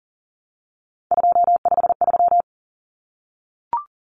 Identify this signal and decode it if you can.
Morse code, heard as a keyed tone, about 40 wpm, 700 Hz. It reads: 253
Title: Numbers - 3-Digit 40wpm